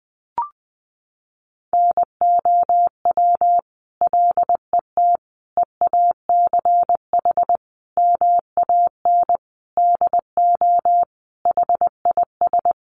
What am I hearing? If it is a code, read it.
Morse code, heard as a keyed tone, about 20 wpm, 700 Hz. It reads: NOW LET EACH MAN DO HIS
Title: Sentences from Top 1500 Words 20wpm